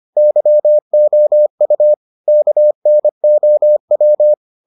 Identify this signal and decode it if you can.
Morse code, heard as a keyed tone, about 25 wpm, 600 Hz. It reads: YOU KNOW